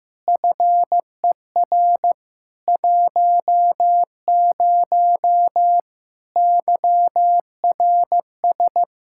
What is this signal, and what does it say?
Morse code, heard as a keyed tone, about 15 wpm, 700 Hz. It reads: FER 10 YRS
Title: Practice QSO - Exchange 3 - More Info - 15wpm